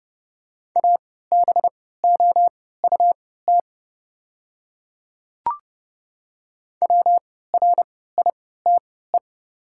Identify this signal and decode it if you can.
Morse code, heard as a keyed tone, about 30 wpm, spaced as for 17 wpm, 690 Hz. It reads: ABOUT WRITE
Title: 3x Character Spacing - Top 100 Words - 30wpm